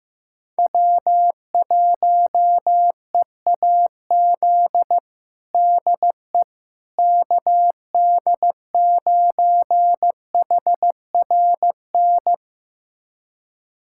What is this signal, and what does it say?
Morse code, heard as a keyed tone, about 15 wpm, 700 Hz. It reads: W1EAZ DE KD9HRN
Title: USA Callsign DE Callsign 15wpm